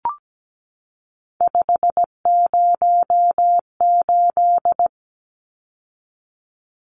Morse code, heard as a keyed tone, about 17 wpm, 700 Hz. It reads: 508
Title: Numbers - 3-Digit 17wpm